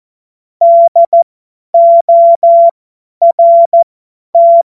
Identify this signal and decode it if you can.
Morse code, heard as a keyed tone, about 14 wpm, spaced as for 10 wpm, 680 Hz. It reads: DORT
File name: Morsekurs / Witze / Witz 04. Oktober 2022 50 F70